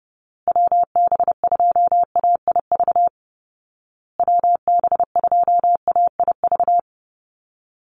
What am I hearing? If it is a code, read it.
Morse code, heard as a keyed tone, about 30 wpm, 700 Hz. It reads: WB2AIV WB2AIV
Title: Call Signs Repeated 2x - 30wpm